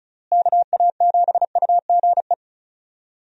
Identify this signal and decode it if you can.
Morse code, heard as a keyed tone, about 34 wpm, 700 Hz. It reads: KA7UGE
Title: USA Call Signs - 6 Characters - Repeated 3x - 35wpm